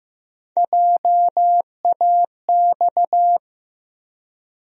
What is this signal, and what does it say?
Morse code, heard as a keyed tone, about 15 wpm, 700 Hz. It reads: JAX
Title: USA Call Sign Postfixes - 3 Letters 15wpm